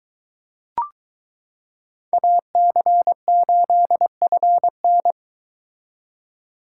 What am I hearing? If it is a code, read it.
Morse code, heard as a keyed tone, about 22 wpm, 700 Hz. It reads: AC8FN